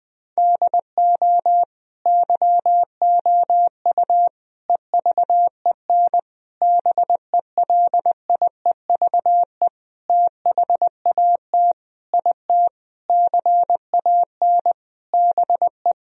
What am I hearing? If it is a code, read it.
Morse code, heard as a keyed tone, about 20 wpm, 700 Hz. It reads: DO YOU EVEN BELIEVE THAT IT CAN BE
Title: Sentences from Top 1000 Words - Review Part 1 - 20wpm